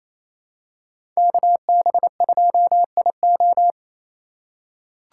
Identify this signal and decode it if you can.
Morse code, heard as a keyed tone, about 28 wpm, 700 Hz. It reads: KB2IO
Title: USA Call Signs - 5 Characters - 6x Character Spacing - 28wpm